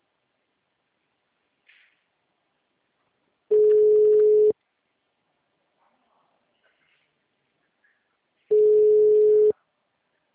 Ringback tone (Europe, 425 Hz).
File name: out- -2020- - - .249